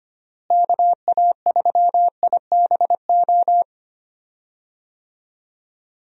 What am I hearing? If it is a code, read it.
Morse code, heard as a keyed tone, about 25 wpm, 700 Hz. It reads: KA3IBO
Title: USA Call Signs - 6 Characters - Repeated 2x - 25wpm